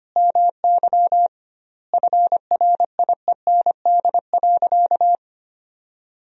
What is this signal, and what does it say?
Morse code, heard as a keyed tone, about 25 wpm, 700 Hz. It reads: MY FRIEND.
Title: 2x Word-Spacing - Sentences from Top 500 Words - 25wpm